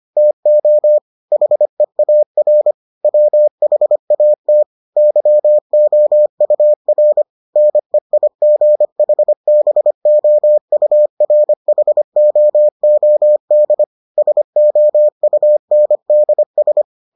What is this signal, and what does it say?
Morse code, heard as a keyed tone, about 25 wpm, 600 Hz. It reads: TO HEAR WHAT YOUR NEIGHBOURHOOD SOUNDS